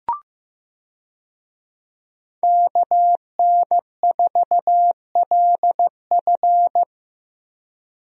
Morse code, heard as a keyed tone, about 15 wpm, 700 Hz. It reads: KN4LF
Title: USA Call Signs - 5 Characters - Repeated 2x - 15wpm